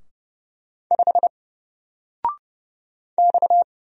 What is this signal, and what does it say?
Morse code, heard as a keyed tone, about 30 wpm, 700 Hz. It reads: 5 X
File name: Single-Letter-Number-30wpm